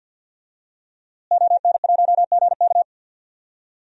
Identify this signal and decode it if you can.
Morse code, heard as a keyed tone, about 50 wpm, 700 Hz. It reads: ON1GK
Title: DXCC-100 Call Signs - 5 Characters - Repeated 2x - 50wpm